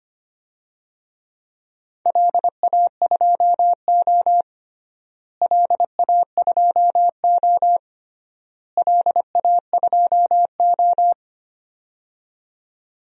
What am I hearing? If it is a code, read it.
Morse code, heard as a keyed tone, about 25 wpm, 700 Hz. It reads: LA2O LA2O LA2O
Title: DXCC-100 Call Signs - 4 Characters - Repeated 3x - 25wpm